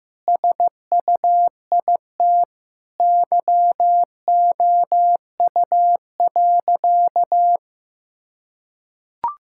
Morse code, heard as a keyed tone, about 15 wpm, 700 Hz. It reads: SUIT YOU.
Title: Sentences from Top 1600 Words 15wpm